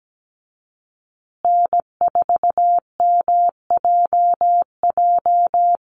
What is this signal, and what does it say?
Morse code, heard as a keyed tone, about 17 wpm, 700 Hz. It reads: N4MJJ
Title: USA Call Signs Repeated 3x 17wpm